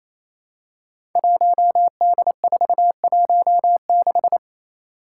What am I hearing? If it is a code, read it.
Morse code, heard as a keyed tone, about 28 wpm, 700 Hz. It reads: 1D416
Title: Five Challenging Letter-Number Combinations 28wpm